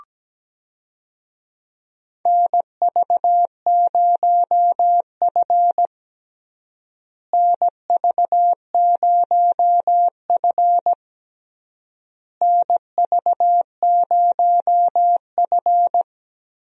Morse code, heard as a keyed tone, about 17 wpm, 700 Hz. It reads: NV0F NV0F NV0F
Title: USA Call Signs - 4 Characters - Repeated 3x - 17wpm